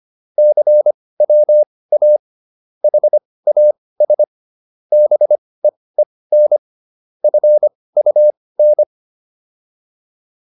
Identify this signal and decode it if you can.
Morse code, heard as a keyed tone, about 25 wpm, spaced as for 18 wpm, 600 Hz. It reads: CWA HAS BEEN FUN
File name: PR305 18